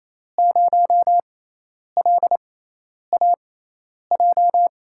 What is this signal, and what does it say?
Morse code, heard as a keyed tone, about 28 wpm, spaced as for 10 wpm, 700 Hz. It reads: 0LAJ
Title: USA Call Signs - 6 Characters - 6x Character Spacing - 28wpm